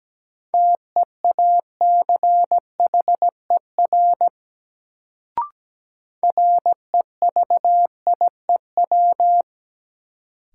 Morse code, heard as a keyed tone, about 17 wpm, 700 Hz. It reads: TEACHER REVIEW
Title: Sentences from Top 900 Words - New Word Drill - 17wpm